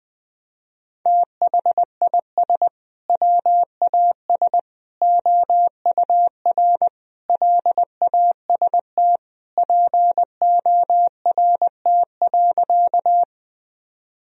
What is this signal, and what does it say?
Morse code, heard as a keyed tone, about 21 wpm, 700 Hz. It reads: THIS WAS OUR LAST PORT.
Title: Sentences from Top 1000 Words - Review Part 1 - 20wpm